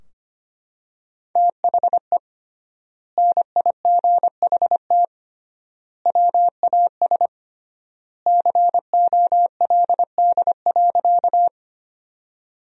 Morse code, heard as a keyed tone, about 25 wpm, 700 Hz. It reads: THE NIGHT WAS COLD.
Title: 3x Word-Spacing - Sentences from Top 400 Words - 25wpm